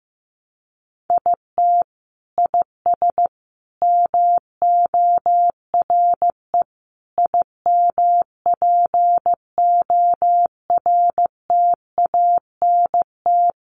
Morse code, heard as a keyed tone, about 15 wpm, 700 Hz. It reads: IT IS MORE IMPORTANT
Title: Sentences from Top 600 Words 15wpm